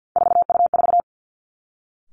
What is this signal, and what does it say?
Morse code, heard as a keyed tone, about 50 wpm, 700 Hz. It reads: 4U4
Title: Three Challenging Letter-Number Combinations 50wpm